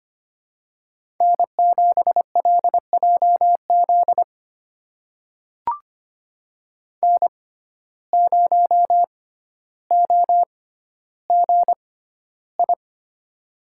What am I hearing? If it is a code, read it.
Morse code, heard as a keyed tone, about 25 wpm, spaced as for 13 wpm, 700 Hz. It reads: N7LJZ N0OGI
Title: USA Call Signs - 5 Characters - 6x Character Spacing - 25wpm